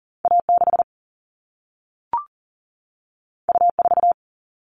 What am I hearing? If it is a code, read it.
Morse code, heard as a keyed tone, about 40 wpm, 700 Hz. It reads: A6 U4
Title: All Cut Numbers - 2-Digit 40wpm